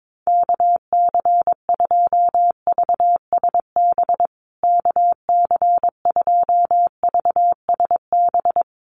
Morse code, heard as a keyed tone, about 22 wpm, 700 Hz. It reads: KC2VSB KC2VSB
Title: USA Call Signs Repeated 3x 22wpm